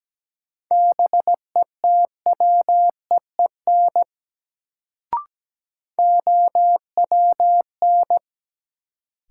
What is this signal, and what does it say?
Morse code, heard as a keyed tone, about 17 wpm, 700 Hz. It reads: BETWEEN OWN
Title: Sets of 2 Words - New Word Drill - 17wpm